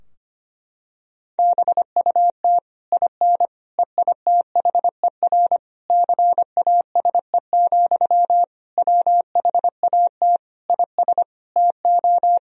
Morse code, heard as a keyed tone, about 25 wpm, 700 Hz. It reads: BUT IN EITHER CASE, WHAT IS TO
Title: Sentences from Top 1000 Words - Review Part 4 - 25wpm